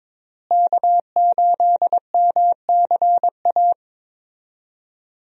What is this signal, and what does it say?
Morse code, heard as a keyed tone, about 23 wpm, 700 Hz. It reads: K8MCA
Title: USA Call Signs - 5 Characters - Repeated 2x - 22wpm